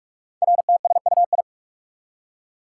Morse code, heard as a keyed tone, about 44 wpm, 700 Hz. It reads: RTSUI
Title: Code Groups - 5 Letters - 45wpm